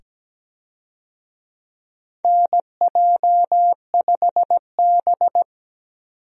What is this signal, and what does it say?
Morse code, heard as a keyed tone, about 17 wpm, 700 Hz. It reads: NJ5B
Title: USA Call Signs - 4 Characters - 17wpm